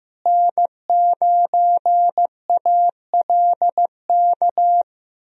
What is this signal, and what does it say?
Morse code, heard as a keyed tone, about 15 wpm, 700 Hz. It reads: N9ALK
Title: USA Call Signs - 5 Characters - Repeated 3x - 15wpm